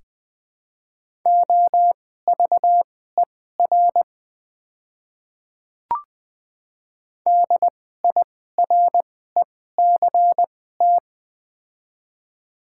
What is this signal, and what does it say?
Morse code, heard as a keyed tone, about 20 wpm, spaced as for 14 wpm, 700 Hz. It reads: OVER DIRECT
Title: 2x Character Spacing - Top 500 Words - 20wpm